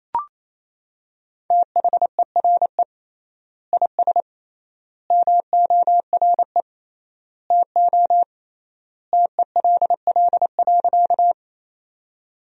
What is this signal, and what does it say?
Morse code, heard as a keyed tone, about 28 wpm, 700 Hz. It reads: THERE IS MORE TO TELL.
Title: 3x Word-Spacing - Sentences from Top 200 Words - 28wpm